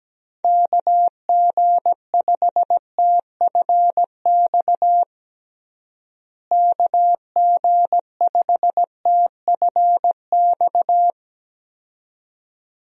Morse code, heard as a keyed tone, about 17 wpm, 700 Hz. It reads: KG5TFX KG5TFX
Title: USA Call Signs - 6 Characters - Repeated 2x - 17wpm